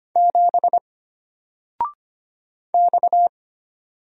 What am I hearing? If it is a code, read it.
Morse code, heard as a keyed tone, about 25 wpm, 700 Hz. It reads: 7 X
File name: Single-Letter-Number-25wpm